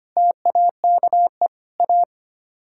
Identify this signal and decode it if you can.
Morse code, heard as a keyed tone, about 30 wpm, 700 Hz. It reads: TAKE A
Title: Speed-Racing - Sets of 4 Words 25wpm